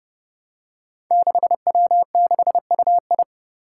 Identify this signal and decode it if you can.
Morse code, heard as a keyed tone, about 28 wpm, 700 Hz. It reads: 6W6UI
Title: Five Challenging Letter-Number Combinations 30wpm